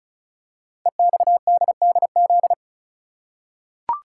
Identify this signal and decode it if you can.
Morse code, heard as a keyed tone, about 34 wpm, 690 Hz. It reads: EXDDZ
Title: Code Groups - 5 Letters - 35wpm